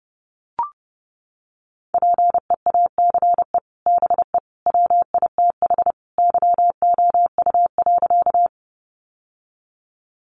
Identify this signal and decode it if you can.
Morse code, heard as a keyed tone, about 30 wpm, 700 Hz. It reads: PEACE BE WITH YOU.
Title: Sentences from Top 1500 Words 30wpm